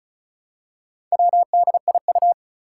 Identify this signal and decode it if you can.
Morse code, heard as a keyed tone, about 36 wpm, 690 Hz. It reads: WDIU
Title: Four Challenging Letter-Number Combinations 35wpm